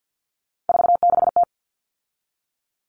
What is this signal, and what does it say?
Morse code, heard as a keyed tone, about 50 wpm, 700 Hz. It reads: 46T
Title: All Cut Numbers - 3-Digit 50wpm